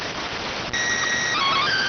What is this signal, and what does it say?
SSTV (slow-scan television), PD 290.